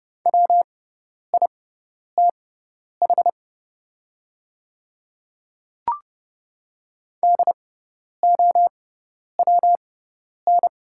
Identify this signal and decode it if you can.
Morse code, heard as a keyed tone, about 30 wpm, spaced as for 10 wpm, 700 Hz. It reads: WITH DOWN
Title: 6x Character Spacing - Top 100 Words - 30wpm